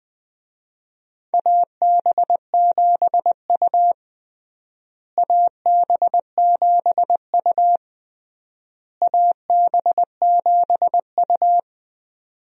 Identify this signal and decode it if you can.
Morse code, heard as a keyed tone, about 20 wpm, 700 Hz. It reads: AB7U AB7U AB7U
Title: USA Call Signs - 4 Characters - Repeated 3x - 20wpm